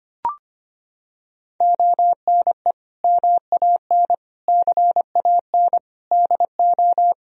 Morse code, heard as a keyed tone, about 25 wpm, 700 Hz. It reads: ONE MAN CAN DO
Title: Sentences from Top 1000 Words - Review Part 4 - 25wpm